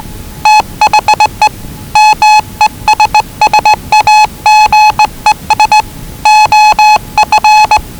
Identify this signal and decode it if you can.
Morse code, heard as a keyed tone, about 20 wpm, 840 Hz. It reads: THE MESSAGES OF
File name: WellingtonRadioZlwCloseDownOnHfAndMf / 01-Zlw 4277khz 417.5 qrt until 0925z